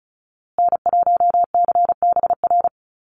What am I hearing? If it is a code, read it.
Morse code, heard as a keyed tone, about 34 wpm, 700 Hz. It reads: N1CBR
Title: USA Call Signs Repeated 3x 35wpm